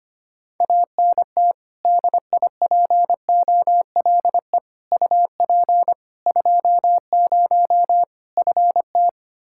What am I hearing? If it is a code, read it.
Morse code, heard as a keyed tone, about 25 wpm, 700 Hz. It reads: ANT DIPOLE UP 20 FT
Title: Practice QSO - Exchange 2 - Info - 25wpm